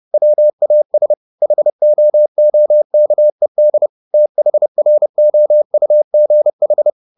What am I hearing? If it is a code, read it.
Morse code, heard as a keyed tone, about 30 wpm, 600 Hz. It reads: WAS HOOKED THROUGH
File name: the old man and the sea / the old man and the sea 1 30wpm